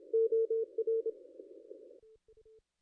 Morse code, heard as a keyed tone, about 29 wpm, 440 Hz. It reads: OR
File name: contests / ss-cw-k5tr-op-n5rz -0600